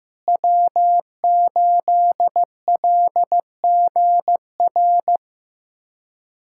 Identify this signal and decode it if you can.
Morse code, heard as a keyed tone, about 15 wpm, 700 Hz. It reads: W8LGR
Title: USA Callsign DE Callsign - 5 Characters - 15wpm